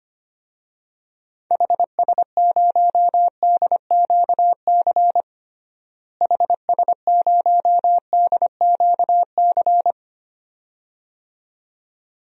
Morse code, heard as a keyed tone, about 25 wpm, 700 Hz. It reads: HS0DQC HS0DQC
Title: DXCC-100 Call Signs - 6 Characters - Repeated 2x - 25wpm